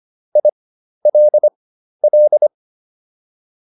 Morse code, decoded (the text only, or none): ILL